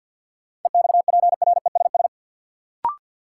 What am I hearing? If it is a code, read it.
Morse code, heard as a keyed tone, about 46 wpm, 700 Hz. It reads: EXPRESS
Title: Sentences from Top 1100 Words - New Word Drill - 50wpm